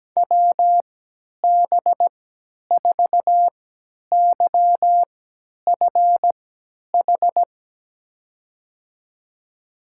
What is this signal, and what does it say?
Morse code, heard as a keyed tone, about 17 wpm, spaced as for 10 wpm, 700 Hz. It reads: WB4YFH
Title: USA Call Signs - 6 Characters - 3x Character Spacing - 17wpm